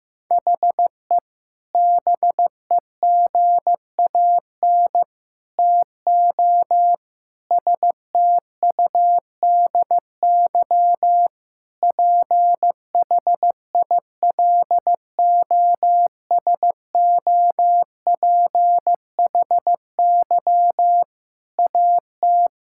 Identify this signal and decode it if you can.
Morse code, heard as a keyed tone, about 15 wpm, 700 Hz. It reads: HE BEGAN TO STUDY PHILOSOPHY AT
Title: Sentences from Top 1000 Words - Review Part 6 - 15wpm